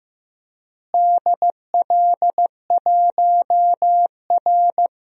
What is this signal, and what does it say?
Morse code, heard as a keyed tone, about 15 wpm, 700 Hz. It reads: DL1R
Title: DXCC-100 Call Signs - 4 Characters - Repeated 3x - 15wpm